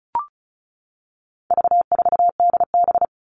Morse code, heard as a keyed tone, about 34 wpm, 700 Hz. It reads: V4DB
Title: All Cut Numbers - 4-Digit 35wpm